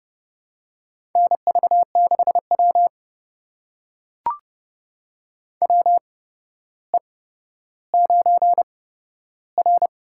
Morse code, heard as a keyed tone, about 30 wpm, spaced as for 13 wpm, 700 Hz. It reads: NV6W WE9R
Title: USA Call Signs - 4 Characters - 8x Character Spacing - 30wpm